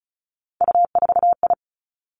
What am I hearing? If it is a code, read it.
Morse code, heard as a keyed tone, about 36 wpm, 700 Hz. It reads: U4I
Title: Three Challenging Letter-Number Combinations 35wpm